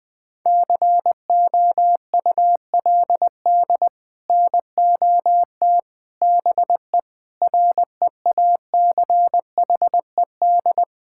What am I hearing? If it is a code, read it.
Morse code, heard as a keyed tone, about 20 wpm, 700 Hz. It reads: COULD NOT BE REACHED
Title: Sets of 6 Words - 20wpm